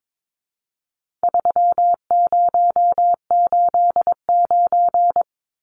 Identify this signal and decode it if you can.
Morse code, heard as a keyed tone, about 22 wpm, 700 Hz. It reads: 3089
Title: Numbers - 4-Digit 22wpm